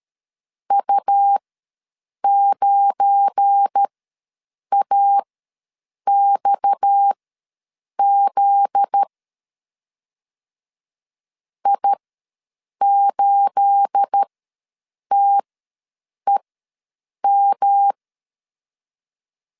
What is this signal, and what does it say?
Morse code, heard as a keyed tone, about 13 wpm, spaced as for 7 wpm, 800 Hz. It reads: U9AXZ I8TEM
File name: LEZIONI / 9 28 ws 12wpm 3354